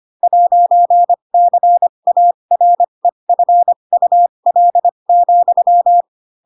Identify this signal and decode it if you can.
Morse code, heard as a keyed tone, about 25 wpm, spaced as for 22 wpm, 700 Hz. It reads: 'CAREFUL,
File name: Pinocchio / Pinocchio ch7 18 25 ik1hge